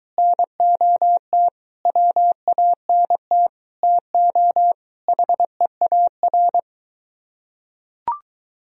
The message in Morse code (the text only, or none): NOT WANT TO HEAR